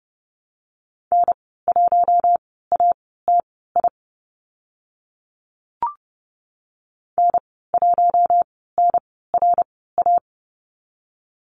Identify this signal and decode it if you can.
Morse code, heard as a keyed tone, about 30 wpm, spaced as for 17 wpm, 700 Hz. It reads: N1ATI N1NRA
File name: Lesson 14g - Number 2 - Call Signs 30wpm